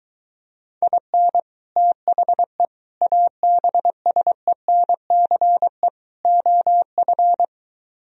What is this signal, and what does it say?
Morse code, heard as a keyed tone, about 23 wpm, 700 Hz. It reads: IN THE ABSENCE OF